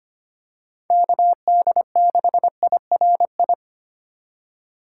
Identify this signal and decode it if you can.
Morse code, heard as a keyed tone, about 25 wpm, 700 Hz. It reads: KD6IRI
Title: K1USN SST QSO Practice - 25wpm